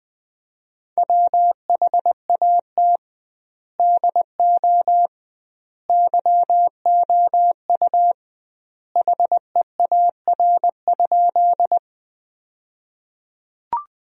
Morse code, heard as a keyed tone, about 20 wpm, 700 Hz. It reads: WHAT DO YOU HEAR?
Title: 2x Word-Spacing - Sentences from Top 300 Words - 20wpm